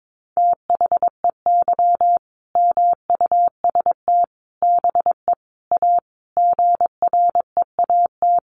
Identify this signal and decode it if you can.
Morse code, heard as a keyed tone, about 22 wpm, 700 Hz. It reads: THEY MUST BE A GREAT
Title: Sentences from Top 1400 Words 22wpm